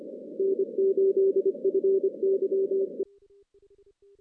Morse code, heard as a keyed tone, about 29 wpm, 380 Hz. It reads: N8FYE
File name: 2016 JAN CW NAQP / 2016 Jan naqp k5ot - 0-11